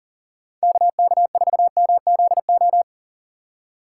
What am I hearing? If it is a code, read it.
Morse code, heard as a keyed tone, about 39 wpm, 700 Hz. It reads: KK4MZO
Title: USA Call Signs - 6 Characters - 4x Character Spacing - 40wpm